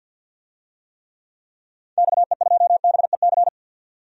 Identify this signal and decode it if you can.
Morse code, heard as a keyed tone, about 53 wpm, 700 Hz. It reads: XE2BEC
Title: DXCC-100 Call Signs - 6 Characters - 50wpm